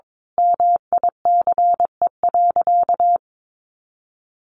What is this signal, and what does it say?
Morse code, heard as a keyed tone, about 22 wpm, 700 Hz. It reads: MICE.
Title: A Sound Of Thunder 22wpm